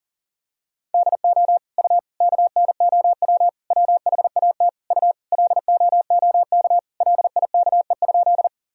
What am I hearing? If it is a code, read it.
Morse code, heard as a keyed tone, about 40 wpm, 700 Hz. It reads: DO U KNOW WHAT U LOOK LIKE?
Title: Sentences from Top 100 Words and Abbreviations 40wpm